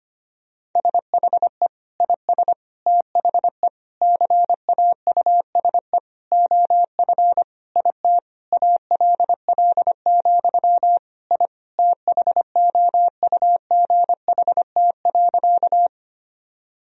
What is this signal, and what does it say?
Morse code, heard as a keyed tone, about 25 wpm, 700 Hz. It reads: SHE IS THE CAUSE OF IT ALL, I THOUGHT.
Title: Sentences from Top 1000 Words - Review Part 1 - 25wpm